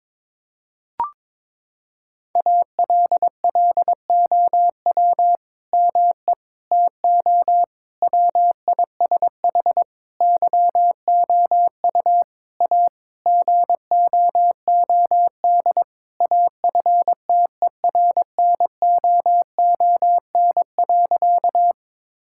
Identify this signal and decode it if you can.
Morse code, heard as a keyed tone, about 22 wpm, 700 Hz. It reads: ALLOW ME TO WISH YOU A GOOD AFTERNOON.
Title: Sentences from Top 1600 Words 22wpm